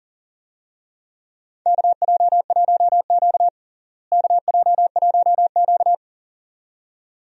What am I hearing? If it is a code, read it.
Morse code, heard as a keyed tone, about 40 wpm, 700 Hz. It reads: KJ1Q KJ1Q
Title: USA Call Signs - 4 Characters - Repeated 2x - 40wpm